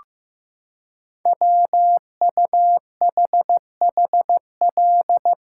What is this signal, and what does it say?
Morse code, heard as a keyed tone, about 15 wpm, 700 Hz. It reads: WUHHL
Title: Code Groups - 5 Letters - 15wpm